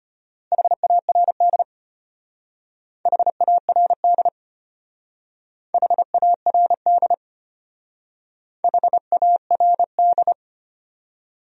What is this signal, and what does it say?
Morse code, heard as a keyed tone, about 31 wpm, 700 Hz. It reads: HARD HARD HARD HARD
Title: Speed-Racing Top 500 Words 25wpm